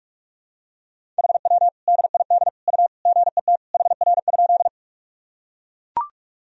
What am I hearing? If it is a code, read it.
Morse code, heard as a keyed tone, about 41 wpm, 700 Hz. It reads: HW DID U GET HR?
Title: Sentences from Top 200 Words and Abbreviations 45wpm